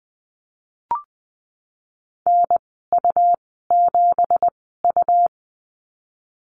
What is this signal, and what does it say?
Morse code, heard as a keyed tone, about 20 wpm, spaced as for 15 wpm, 700 Hz. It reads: NU7U
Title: 173 Lesson 28g - Number 9 - Call Signs 20wpm